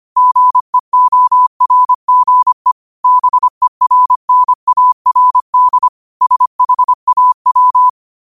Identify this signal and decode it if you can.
Morse code, heard as a keyed tone, about 25 wpm, 1000 Hz. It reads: GEORGE BERNARD SHAW
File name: qotdfiles / 25wpm